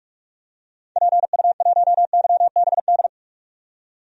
Morse code, heard as a keyed tone, about 46 wpm, 700 Hz. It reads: PU1YBD
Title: DXCC-100 Call Signs - 6 Characters - 45wpm